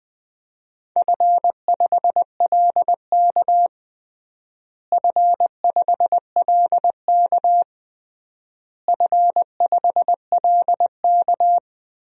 Morse code, heard as a keyed tone, about 20 wpm, 700 Hz. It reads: F5LK F5LK F5LK
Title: DXCC-100 Call Signs - 4 Characters - Repeated 3x - 20wpm